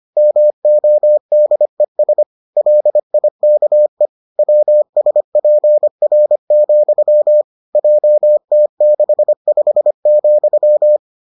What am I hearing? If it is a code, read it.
Morse code, heard as a keyed tone, about 25 wpm, 600 Hz. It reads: MODES LIKE WSPR, JT65,